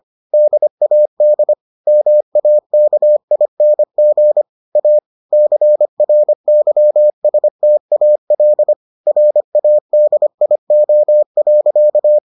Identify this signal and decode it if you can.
Morse code, heard as a keyed tone, about 25 wpm, 600 Hz. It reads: DAD MAKING A CRYSTAL RADIO.